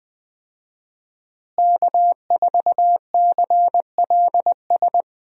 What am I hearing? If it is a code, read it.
Morse code, heard as a keyed tone, about 20 wpm, 700 Hz. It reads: K4CLS
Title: USA Call Signs - 5 Characters - Repeated 2x - 20wpm